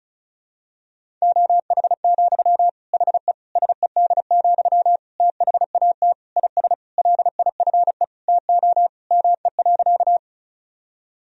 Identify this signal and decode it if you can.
Morse code, heard as a keyed tone, about 34 wpm, 700 Hz. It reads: OH, HE SED, THAT IS LIFE TO ME.
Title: Sentences from Top 500 Words and Abbreviations 35wpm